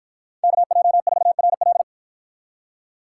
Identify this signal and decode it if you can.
Morse code, heard as a keyed tone, about 53 wpm, 700 Hz. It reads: KJ4RL